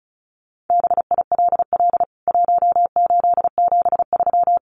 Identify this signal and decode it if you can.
Morse code, heard as a keyed tone, about 34 wpm, 700 Hz. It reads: BILL 1873
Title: CWT Exchange 35wpm